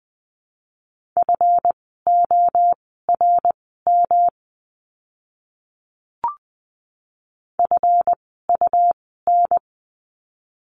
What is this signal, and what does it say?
Morse code, heard as a keyed tone, about 20 wpm, spaced as for 15 wpm, 700 Hz. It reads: FORM FUN
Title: 139 Lesson 23f - Letter f - Words 20wpm